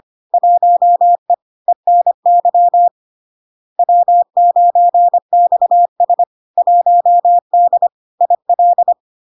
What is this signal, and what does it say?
Morse code, heard as a keyed tone, about 25 wpm, 700 Hz. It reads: 1E ENY W9XS 1D IL
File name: LARK-FD-2022-Log-in-CW-25wpm-700-Hz